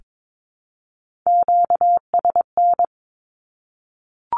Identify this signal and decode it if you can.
Morse code, heard as a keyed tone, about 22 wpm, 700 Hz. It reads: QSN
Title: Q-Codes 22wpm